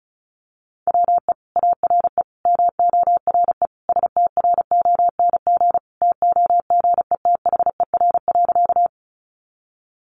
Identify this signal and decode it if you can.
Morse code, heard as a keyed tone, about 34 wpm, 700 Hz. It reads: WE ARE MORE STRONG TOGETHER.
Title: Sentences from Top 500 Words 35wpm